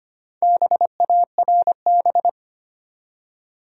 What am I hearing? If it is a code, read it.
Morse code, heard as a keyed tone, about 25 wpm, 700 Hz. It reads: BARB
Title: USA Common Names - Max 4 Letters - 25wpm